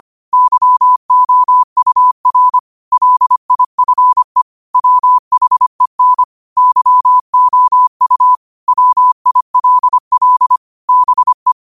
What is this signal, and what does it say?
Morse code, heard as a keyed tone, about 25 wpm, 1000 Hz. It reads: YOUR LIFE WHEN YOU WILL BE
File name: qotdfiles / 25wpm